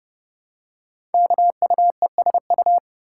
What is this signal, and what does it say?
Morse code, heard as a keyed tone, about 30 wpm, 700 Hz. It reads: KUESU
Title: Code Groups - 5 Letter:Numbers - 30wpm